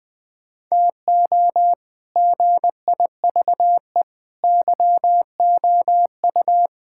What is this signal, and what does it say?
Morse code, heard as a keyed tone, about 20 wpm, 700 Hz. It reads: TO GIVE YOU
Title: Sentences with 6-Word Phrases - 20wpm